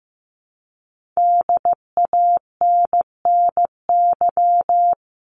Morse code, heard as a keyed tone, about 15 wpm, 700 Hz. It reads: DANNY
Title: CWT Exchange 15wpm